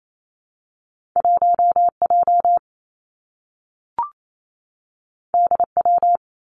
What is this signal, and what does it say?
Morse code, heard as a keyed tone, about 28 wpm, 700 Hz. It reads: 1 J D W
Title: Two Challenging Letter-Number Combinations 28wpm